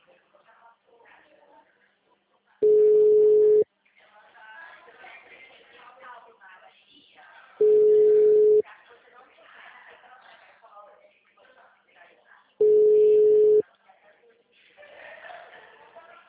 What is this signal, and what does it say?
Ringback tone (Europe, 425 Hz), with speech.
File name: out- -2002- - - .672